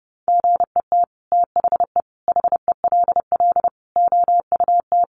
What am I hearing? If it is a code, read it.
Morse code, heard as a keyed tone, about 30 wpm, 700 Hz. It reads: GET THE HELL OUT
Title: Sets of 5 Words Encore2 30wpm